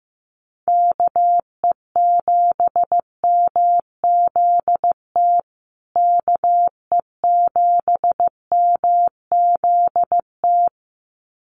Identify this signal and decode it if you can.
Morse code, heard as a keyed tone, about 15 wpm, 700 Hz. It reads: KE7MZT KE7MZT
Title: USA Call Signs Repeated 2x 15wpm